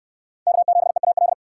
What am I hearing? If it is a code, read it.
Morse code, heard as a keyed tone, about 68 wpm, 700 Hz. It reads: K7EAL